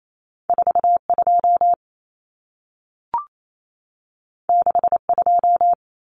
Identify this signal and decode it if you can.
Morse code, heard as a keyed tone, about 28 wpm, 700 Hz. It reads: 42 62
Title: Common Cut Numbers - 2-Digit 28wpm